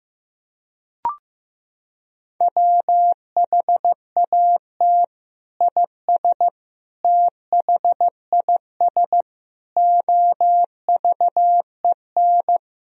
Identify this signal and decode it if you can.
Morse code, heard as a keyed tone, about 15 wpm, 700 Hz. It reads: WHAT IS THIS OVEN
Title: Sentences from Top 1000 Words - Review Part 4 - 15wpm